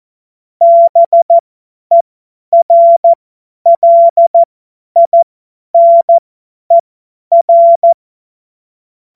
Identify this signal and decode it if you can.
Morse code, heard as a keyed tone, about 14 wpm, spaced as for 10 wpm, 680 Hz. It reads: BERLINER
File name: Witz-19-August-2025-50-F70